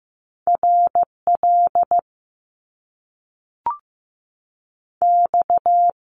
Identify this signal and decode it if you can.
Morse code, heard as a keyed tone, about 15 wpm, 700 Hz. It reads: R L X
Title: Top Two Letter Combinations in Words 15wpm